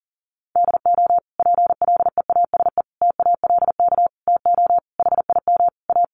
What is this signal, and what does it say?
Morse code, heard as a keyed tone, about 40 wpm, 700 Hz. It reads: DO PLEASE TALK TO HIM A
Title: Sentences from Top 1500 Words 40wpm